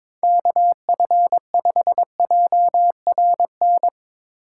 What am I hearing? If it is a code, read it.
Morse code, heard as a keyed tone, about 23 wpm, 700 Hz. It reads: KF5JRN
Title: Speed Racing - USA 2x3 Call Signs 22wpm